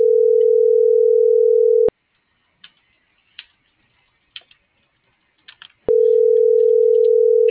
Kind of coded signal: telephone call progress tone

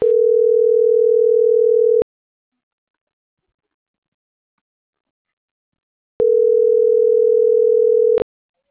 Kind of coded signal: telephone call progress tone

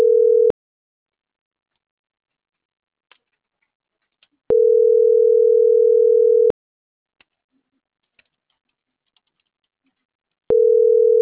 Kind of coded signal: telephone call progress tone

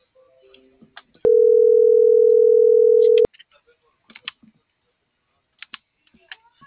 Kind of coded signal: telephone call progress tone

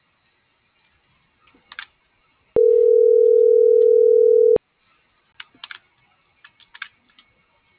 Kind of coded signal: telephone call progress tone

